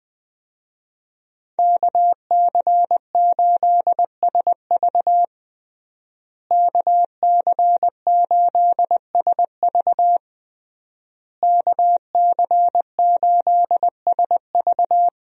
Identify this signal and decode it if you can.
Morse code, heard as a keyed tone, about 21 wpm, 700 Hz. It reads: KC8SV KC8SV KC8SV
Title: USA Call Signs - 5 Characters - Repeated 3x - 20wpm